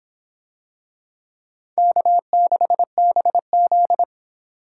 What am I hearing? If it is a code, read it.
Morse code, heard as a keyed tone, about 26 wpm, 700 Hz. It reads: K6BZ